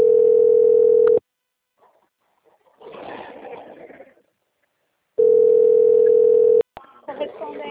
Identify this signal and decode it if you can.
Ringback tone (North America), 440 + 480 Hz, with speech.
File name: out- - - - - .87945